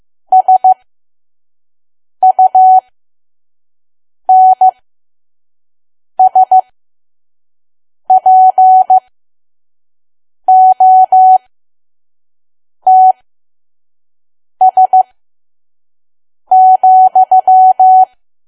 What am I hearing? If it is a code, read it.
Morse code, heard as a keyed tone, about 15 wpm, spaced as for 5 wpm, 750 Hz. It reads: SUNSPOTS,
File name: Archive / 5 WPM files / PM